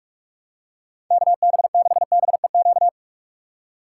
Morse code, heard as a keyed tone, about 46 wpm, 700 Hz. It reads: KB6BEQ
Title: USA Call Signs - 6x Character Spacing - 45wpm